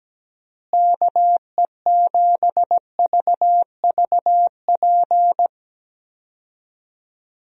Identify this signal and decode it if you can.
Morse code, heard as a keyed tone, about 17 wpm, 700 Hz. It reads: KE7VVP
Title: USA Call Signs - 6 Characters - Repeated 3x - 17wpm